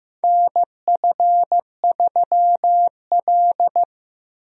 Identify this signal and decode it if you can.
Morse code, heard as a keyed tone, about 15 wpm, 700 Hz. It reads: NF3L
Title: USA Call Signs - 4 Characters - Repeated 2x - 15wpm